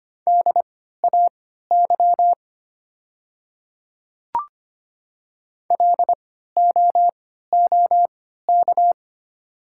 Morse code, heard as a keyed tone, about 25 wpm, spaced as for 14 wpm, 700 Hz. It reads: DAY LOOK
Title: 3x Character Spacing - Top 100 Words - 25wpm